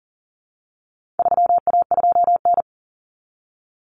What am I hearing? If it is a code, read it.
Morse code, heard as a keyed tone, about 40 wpm, 700 Hz. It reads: 3A2N
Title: Common Cut Numbers - 4-Digit 40wpm